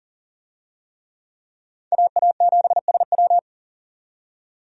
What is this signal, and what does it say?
Morse code, heard as a keyed tone, about 40 wpm, 700 Hz. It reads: AA7SW
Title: USA Call Signs - 5 Characters - 40wpm